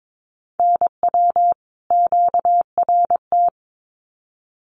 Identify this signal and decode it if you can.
Morse code, heard as a keyed tone, about 22 wpm, 700 Hz. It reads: NW QRT
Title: Common QSO Phrases 22wpm